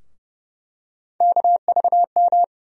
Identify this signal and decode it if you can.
Morse code, heard as a keyed tone, about 30 wpm, 700 Hz. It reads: KVM
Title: USA Call Sign Postfixes - 3 Letters 30wpm